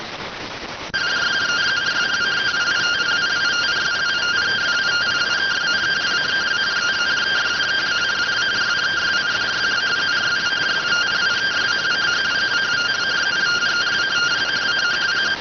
Two-tone teleprinter signal, about 170 Hz shift.